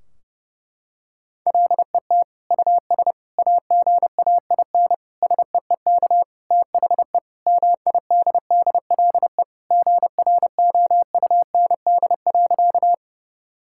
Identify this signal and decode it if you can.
Morse code, heard as a keyed tone, about 32 wpm, 700 Hz. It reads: LET US AGAIN SEEK THE MIDDLE GROUND.
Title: Sentences from Top 1600 Words 30wpm